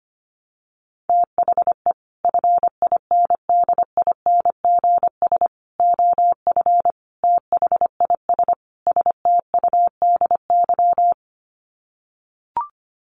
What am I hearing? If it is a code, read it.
Morse code, heard as a keyed tone, about 25 wpm, 700 Hz. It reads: THE FINDINGS OF THIS STUDY
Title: Sets of 5 Words Encore2 25wpm